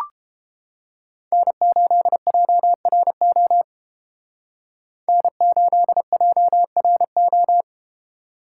Morse code, heard as a keyed tone, about 31 wpm, 700 Hz. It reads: N8JRO N8JRO